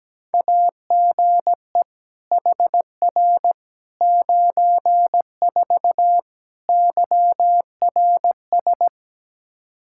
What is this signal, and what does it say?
Morse code, heard as a keyed tone, about 17 wpm, 700 Hz. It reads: AGE HR 94 YRS
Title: K1USN SST QSO Practice 17wpm